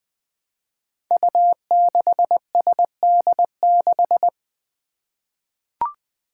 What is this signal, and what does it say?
Morse code, heard as a keyed tone, about 20 wpm, 700 Hz. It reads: U6SD6
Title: Five Challenging Letter-Number Combinations 20wpm